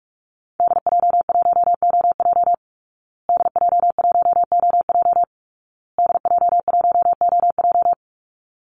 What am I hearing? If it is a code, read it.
Morse code, heard as a keyed tone, about 44 wpm, 700 Hz. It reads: DJ1OJ DJ1OJ DJ1OJ
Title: Call Signs Repeated 2x - 45wpm